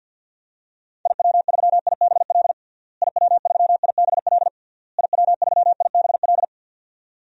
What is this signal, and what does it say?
Morse code, heard as a keyed tone, about 49 wpm, 700 Hz. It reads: IW3IBL IW3IBL IW3IBL
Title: DXCC-100 Call Signs - 6 Characters - Repeated 3x - 50wpm